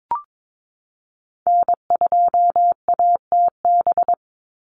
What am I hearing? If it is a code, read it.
Morse code, heard as a keyed tone, about 22 wpm, 700 Hz. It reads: N2ATB
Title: Call Signs Repeated 3x - 22wpm